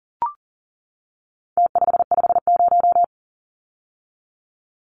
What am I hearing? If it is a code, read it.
Morse code, heard as a keyed tone, about 40 wpm, 700 Hz. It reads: T550
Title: Numbers - Contest Style - 4-Digit 40wpm